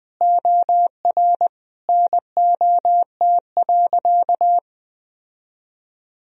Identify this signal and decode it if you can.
Morse code, heard as a keyed tone, about 20 wpm, 700 Hz. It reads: OR NOT.
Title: Sentences from Top 1000 Words - Review Part 3 - 20wpm